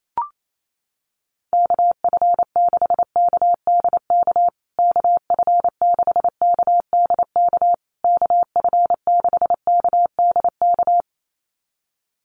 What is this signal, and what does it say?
Morse code, heard as a keyed tone, about 28 wpm, 700 Hz. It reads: KF6KDK KF6KDK KF6KDK
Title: USA Call Signs Repeated 3x 28wpm